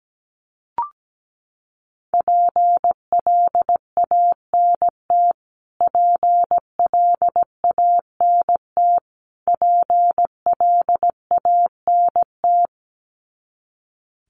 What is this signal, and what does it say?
Morse code, heard as a keyed tone, about 17 wpm, 700 Hz. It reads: PLANT PLANT PLANT
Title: Top 500 Words Repeated 17wpm